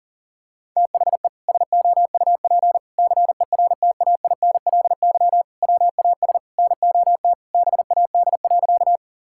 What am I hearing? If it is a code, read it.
Morse code, heard as a keyed tone, about 40 wpm, 700 Hz. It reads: THE SOUP CERTAINLY WAS NOT BAD.
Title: Sentences from Top 1000 Words - Review Part 6 - 40wpm